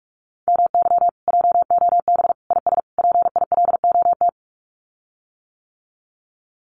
Morse code, heard as a keyed tone, about 46 wpm, 700 Hz. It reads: MY JOB IS PILOT
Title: Common QSO Phrases Encore 45wpm